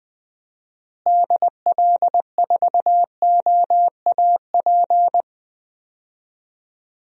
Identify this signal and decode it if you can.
Morse code, heard as a keyed tone, about 20 wpm, 700 Hz. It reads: DL4OAP
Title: USA Call Signs - 6 Characters - Repeated 3x - 20wpm